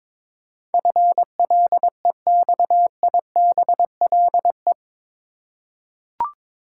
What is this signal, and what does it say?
Morse code, heard as a keyed tone, about 21 wpm, 700 Hz. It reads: FLEXIBLE
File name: practice / 22wpm / Eight Letter Words - 22wpm